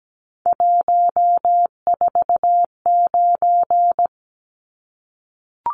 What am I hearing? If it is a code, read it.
Morse code, heard as a keyed tone, about 17 wpm, 700 Hz. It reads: 149
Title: Numbers - 3-Digit 17wpm